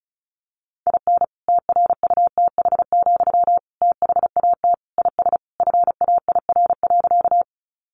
Morse code, heard as a keyed tone, about 34 wpm, 700 Hz. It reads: IN TRUTH, THAT IS FAIR.
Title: Sentences from Top 1300 Words 35wpm